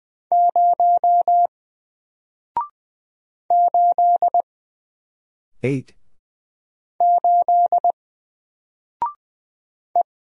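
Morse code, heard as a keyed tone, about 20 wpm, 700 Hz. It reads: 0 8 8 E
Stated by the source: Single letter and number Morse drill at 20 wpm, from N4LCD.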